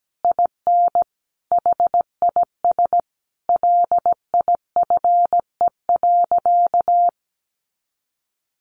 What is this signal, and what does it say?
Morse code, heard as a keyed tone, about 17 wpm, 700 Hz. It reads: IN HIS LIFE.
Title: Sentences from Top 1100 Words 17wpm